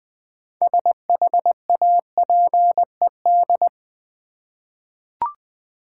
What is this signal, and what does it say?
Morse code, heard as a keyed tone, about 20 wpm, 700 Hz. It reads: SHAPED
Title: 6 Letter Words - First 3 Letters Given - 20wpm